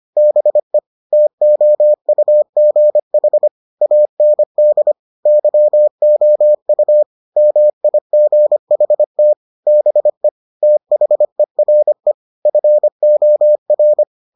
Morse code, heard as a keyed tone, about 25 wpm, 600 Hz. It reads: BE TOUGH AND YOU MIGHT BE THERE FOR